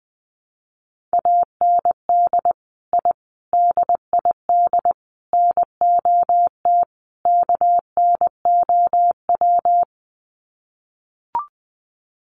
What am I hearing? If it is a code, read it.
Morse code, heard as a keyed tone, about 20 wpm, 700 Hz. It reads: AND I DID NOT KNOW
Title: Sets of 5 Words Encore 20wpm